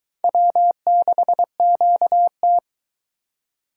Morse code, heard as a keyed tone, about 22 wpm, 700 Hz. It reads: W6QT